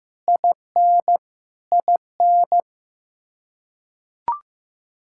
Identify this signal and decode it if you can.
Morse code, heard as a keyed tone, about 15 wpm, 700 Hz. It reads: IN IN
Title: Practice QSO - Exchange 1 - Beginning - 15wpm